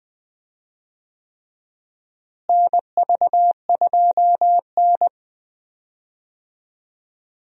Morse code, heard as a keyed tone, about 20 wpm, 700 Hz. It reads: NV2N
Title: USA Call Signs - 4 Characters - Repeated 3x - 20wpm